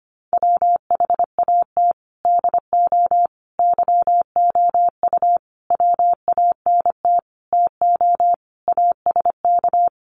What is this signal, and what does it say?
Morse code, heard as a keyed tone, about 25 wpm, 700 Hz. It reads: WHAT DO YOU WANT TO ASK
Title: Sentences from Top 200 Words 25wpm